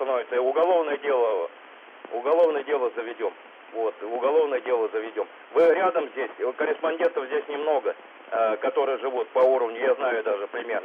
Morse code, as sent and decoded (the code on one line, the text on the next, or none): none